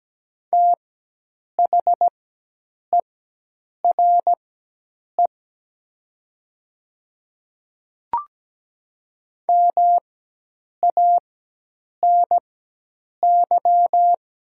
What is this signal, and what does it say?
Morse code, heard as a keyed tone, about 17 wpm, spaced as for 8 wpm, 700 Hz. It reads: THERE MANY
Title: 4x Character Spacing - Top 200 Words - 17wpm